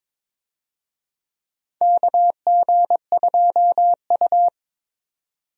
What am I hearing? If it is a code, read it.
Morse code, heard as a keyed tone, about 22 wpm, 700 Hz. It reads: KG2U
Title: USA Call Signs - 4 Characters - 22wpm